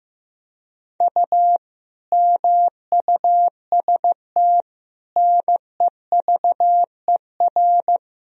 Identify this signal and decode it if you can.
Morse code, heard as a keyed tone, about 15 wpm, 700 Hz. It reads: U MUST NEVER
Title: Sentences from Top 300 Words and Abbreviations 15wpm